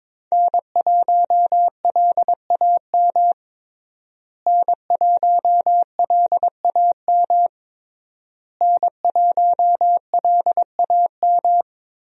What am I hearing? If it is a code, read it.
Morse code, heard as a keyed tone, about 21 wpm, 700 Hz. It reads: N1LAM N1LAM N1LAM
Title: USA Call Signs - 5 Characters - Repeated 3x - 22wpm